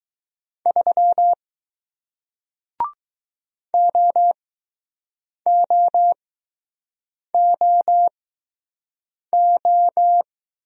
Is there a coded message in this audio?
Yes: Morse code from 0.7 to 10.2 s, heard as a keyed tone, about 20 wpm, spaced as for 6 wpm, 700 Hz. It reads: 3 OOOO